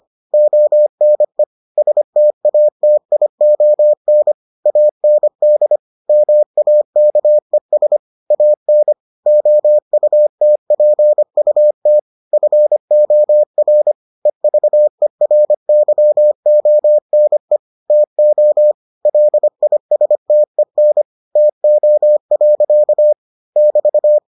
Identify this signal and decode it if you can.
Morse code, heard as a keyed tone, about 25 wpm, 600 Hz. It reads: ONE STATION AND MAKES AN OUTPUT FOR EVERYONE TO LISTEN TO. <BT>